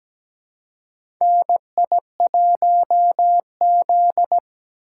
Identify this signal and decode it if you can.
Morse code, heard as a keyed tone, about 17 wpm, 700 Hz. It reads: NI1Z
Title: USA Call Signs - 4 Characters - 4x Character Spacing - 17wpm